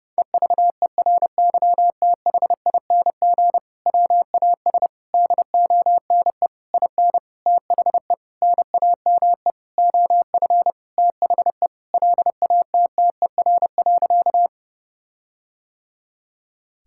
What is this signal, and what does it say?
Morse code, heard as a keyed tone, about 30 wpm, 700 Hz. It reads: EVERYTHING WAS DONE IN THE NAME OF THE LATTER.
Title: Sentences from Top 1600 Words 30wpm